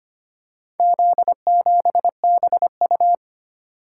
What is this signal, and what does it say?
Morse code, heard as a keyed tone, about 25 wpm, 700 Hz. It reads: Z7BU
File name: practice / 25wpm / Four Letter-Number Combinations - 25wpm